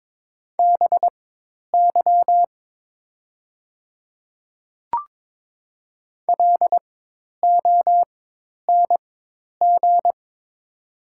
Morse code, heard as a keyed tone, about 22 wpm, spaced as for 10 wpm, 700 Hz. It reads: BY LONG
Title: 4x Character Spacing - Top 200 Words - 22wpm